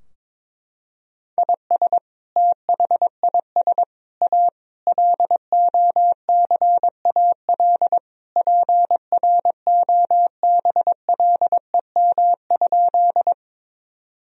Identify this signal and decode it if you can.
Morse code, heard as a keyed tone, about 22 wpm, 700 Hz. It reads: IS THIS A LOCAL PROBLEM?
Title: Sentences from Top 1000 Words - Review Part 3 - 22wpm